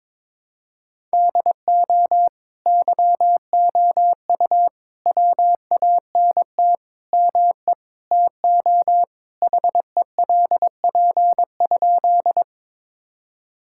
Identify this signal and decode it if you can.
Morse code, heard as a keyed tone, about 22 wpm, 700 Hz. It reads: DO YOU WANT ME TO HELP?
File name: practice / 22wpm / Sentences with 4-Word Phrases - 22wpm